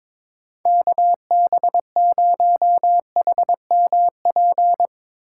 Morse code, heard as a keyed tone, about 21 wpm, 700 Hz. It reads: KB0HMP
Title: USA Call Signs - 3x Character Spacing - 22wpm